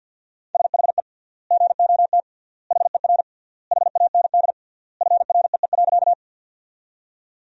Morse code, heard as a keyed tone, about 48 wpm, 700 Hz. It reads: SHE GOT HER HAND FREE.
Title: 3x Word-Spacing - Sentences from Top 500 Words - 50wpm